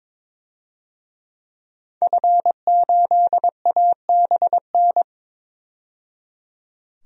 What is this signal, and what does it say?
Morse code, heard as a keyed tone, about 22 wpm, 700 Hz. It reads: F8ABN
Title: DXCC-100 Call Signs - 5 Characters - 22wpm